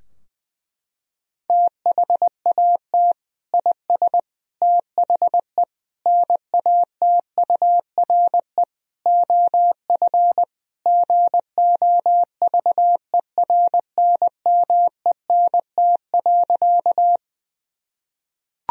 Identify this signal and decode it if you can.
Morse code, heard as a keyed tone, about 21 wpm, 700 Hz. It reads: THAT IS THE NATURE OF GOVERNMENT.
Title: Sentences from Top 1000 Words - Review Part 3 - 20wpm